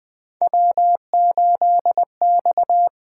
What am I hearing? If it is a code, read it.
Morse code, heard as a keyed tone, about 20 wpm, 700 Hz. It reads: W8X
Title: USA Call Signs - 4 Characters - 8x Character Spacing - 20wpm